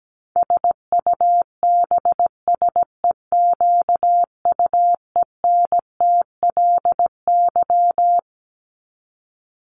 Morse code, heard as a keyed tone, about 17 wpm, 700 Hz. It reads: SUBSEQUENTLY
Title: LY Suffix - Top 100 Words with Base Words 17wpm